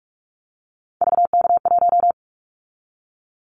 Morse code, heard as a keyed tone, about 46 wpm, 700 Hz. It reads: VK1
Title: Callsign Prefixes for Top 100 DXCC - 45wpm